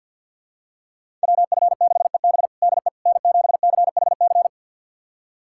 Morse code, heard as a keyed tone, about 51 wpm, 700 Hz. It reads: WF6EB DE N7CHC
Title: USA Callsign DE Callsign - 5 Characters - 50wpm